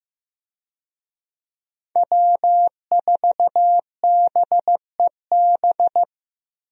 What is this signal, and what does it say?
Morse code, heard as a keyed tone, about 16 wpm, 700 Hz. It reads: W4BEB
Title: USA Call Signs - 5 Characters - Repeated 2x - 15wpm